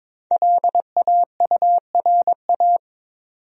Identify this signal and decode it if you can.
Morse code, heard as a keyed tone, about 22 wpm, 700 Hz. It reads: LAURA
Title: USA Common Names - 22wpm